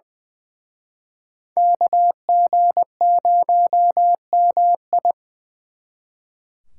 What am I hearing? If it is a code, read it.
Morse code, heard as a keyed tone, about 24 wpm, 700 Hz. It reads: KG0MI
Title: Speed Racing - USA Call Signs 20wpm